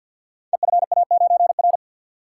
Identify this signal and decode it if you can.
Morse code, heard as a keyed tone, about 49 wpm, 700 Hz. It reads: EFA9R
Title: Code Groups - 5 Letter:Numbers - 50wpm